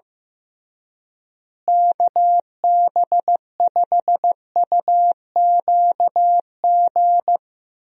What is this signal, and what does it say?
Morse code, heard as a keyed tone, about 15 wpm, 700 Hz. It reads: KB5UQG
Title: USA Call Signs - 6 Characters - Repeated 3x - 15wpm